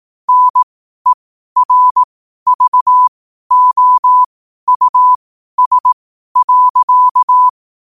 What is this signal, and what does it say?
Morse code, heard as a keyed tone, about 18 wpm, spaced as for 12 wpm, 1000 Hz. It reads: NERVOUS.
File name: qotdfiles / 12wpm